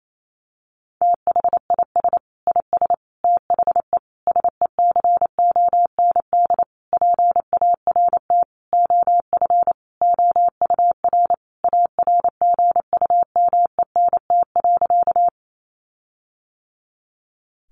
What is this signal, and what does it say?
Morse code, heard as a keyed tone, about 28 wpm, 700 Hz. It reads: THIS IS THE SECOND PART OF OUR ARGUMENT.
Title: Sentences from Top 1400 Words 28wpm